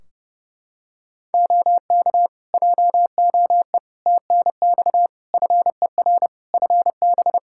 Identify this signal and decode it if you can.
Morse code, heard as a keyed tone, about 30 wpm, 700 Hz. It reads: OK JOE TNX FER FB
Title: Practice QSO - Exchange 4 - Ending - 30wpm